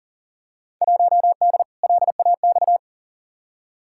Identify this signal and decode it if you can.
Morse code, heard as a keyed tone, about 40 wpm, 700 Hz. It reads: 1D LAX
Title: Field Day Exchanges - 40wpm